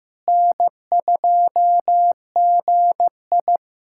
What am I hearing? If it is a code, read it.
Morse code, heard as a keyed tone, about 15 wpm, 700 Hz. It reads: N2GI
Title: USA Callsign DE Callsign - 4 Characters - 15wpm